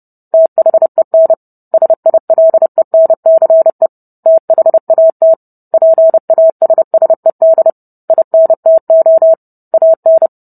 Morse code, heard as a keyed tone, about 30 wpm, 650 Hz. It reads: THEN SILENCE THAT PASSED INTO AN